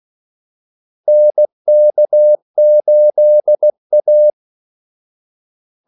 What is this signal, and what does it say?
Morse code, heard as a keyed tone, about 16 wpm, 600 Hz. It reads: NK8A
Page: Kaldesignaler 16 wpm